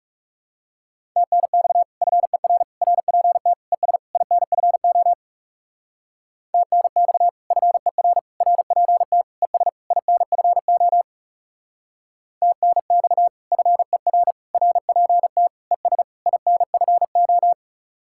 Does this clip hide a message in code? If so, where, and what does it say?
Yes: Morse code from 1.2 to 17.5 s, heard as a keyed tone, about 38 wpm, 700 Hz. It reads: TNX FER RPT ES INFO TNX FER RPT ES INFO TNX FER RPT ES INFO